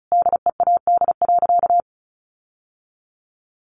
Morse code, heard as a keyed tone, about 34 wpm, 700 Hz. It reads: DEAD.
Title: Sentences from Top 1500 Words 35wpm